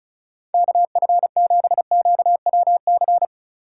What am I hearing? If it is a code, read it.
Morse code, heard as a keyed tone, about 36 wpm, 700 Hz. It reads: KF7QWC
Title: USA Call Signs - 6 Characters - 2x Character Spacing - 35wpm